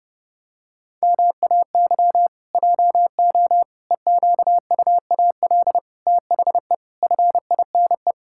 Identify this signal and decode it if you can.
Morse code, heard as a keyed tone, about 28 wpm, 700 Hz. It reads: MAY JO EQUAL THE FINE
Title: Pangram Challenge v2 - 30wpm